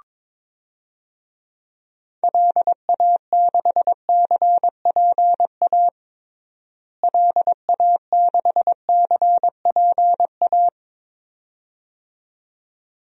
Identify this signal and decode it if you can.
Morse code, heard as a keyed tone, about 22 wpm, 700 Hz. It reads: LA6CPA LA6CPA
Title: DXCC-100 Call Signs - 6 Characters - Repeated 2x - 22wpm